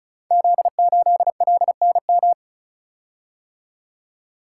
Morse code, heard as a keyed tone, about 36 wpm, 690 Hz. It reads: Z8LNM
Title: Code Groups - 5 Letter:Numbers - 35wpm